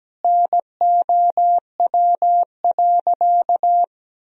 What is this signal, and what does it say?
Morse code, heard as a keyed tone, about 17 wpm, 700 Hz. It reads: NOW.
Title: Sentences from Top 1000 Words - Review Part 1 - 17wpm